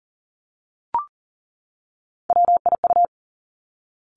Morse code, heard as a keyed tone, about 40 wpm, 700 Hz. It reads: WIU
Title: Three Challenging Letter-Number Combinations 40wpm